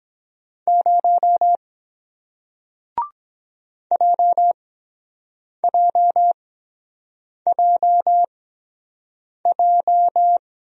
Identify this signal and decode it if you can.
Morse code, heard as a keyed tone, about 23 wpm, spaced as for 7 wpm, 700 Hz. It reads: 0 JJJJ